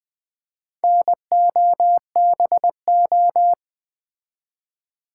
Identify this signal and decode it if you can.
Morse code, heard as a keyed tone, about 20 wpm, 700 Hz. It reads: NOBO
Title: Build A Word - 2 Letters At A Time - 6-Letter Words - 20wpm